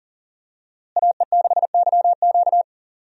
Morse code, heard as a keyed tone, about 38 wpm, 700 Hz. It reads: AE6YQ
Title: USA Call Signs - 5 Characters - Repeated 3x - 40wpm